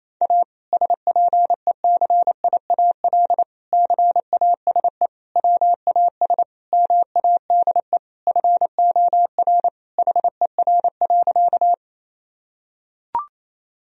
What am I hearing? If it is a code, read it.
Morse code, heard as a keyed tone, about 28 wpm, 700 Hz. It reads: A SPECIAL CASE WAS MADE FOR HER.
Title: Sentences from Top 1000 Words - Review Part 3 - 28wpm